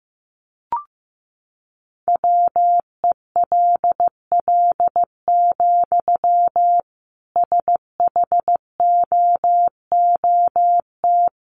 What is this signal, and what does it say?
Morse code, heard as a keyed tone, about 15 wpm, 700 Hz. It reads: WELL, SHOOT
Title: Sentences from Top 1200 Words 15wpm